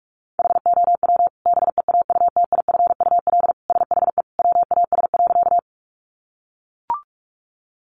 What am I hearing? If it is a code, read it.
Morse code, heard as a keyed tone, about 44 wpm, 700 Hz. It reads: HOW BEAUTIFUL SHE WAS.
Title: Sentences from Top 900 Words 45wpm